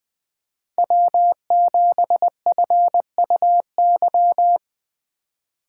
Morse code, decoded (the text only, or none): W7FUY